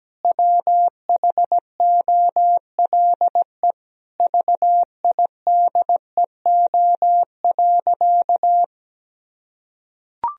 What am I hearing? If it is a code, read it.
Morse code, heard as a keyed tone, about 17 wpm, 700 Hz. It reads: WHOLE VIDEO.
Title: Sentences from Top 1000 Words - Review Part 4 - 17wpm